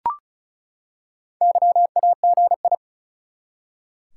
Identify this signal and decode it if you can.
Morse code, heard as a keyed tone, about 36 wpm, 700 Hz. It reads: YAGI
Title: Top 180 QSO Elements 35wpm